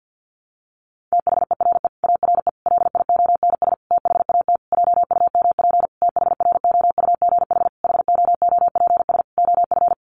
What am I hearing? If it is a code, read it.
Morse code, heard as a keyed tone, about 50 wpm, 700 Hz. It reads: THERE ARE LIONS THAT JUMP THROUGH HOOPS OF
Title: Easy Phrases 50wpm